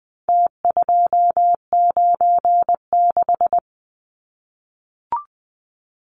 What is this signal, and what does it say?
Morse code, heard as a keyed tone, about 20 wpm, 700 Hz. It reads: T296
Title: Numbers - Contest Style - 4-Digit 20wpm